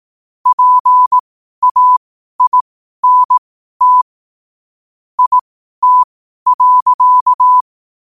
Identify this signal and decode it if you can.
Morse code, heard as a keyed tone, about 18 wpm, spaced as for 12 wpm, 1000 Hz. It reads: PAINT IT.
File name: qotdfiles / 12wpm